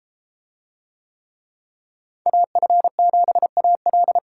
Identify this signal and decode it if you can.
Morse code, heard as a keyed tone, about 31 wpm, 700 Hz. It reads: AF7AL